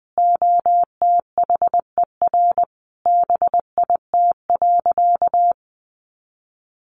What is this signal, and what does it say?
Morse code, heard as a keyed tone, about 20 wpm, 700 Hz. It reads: OTHER BIT.
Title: Sentences from Top 900 Words 20wpm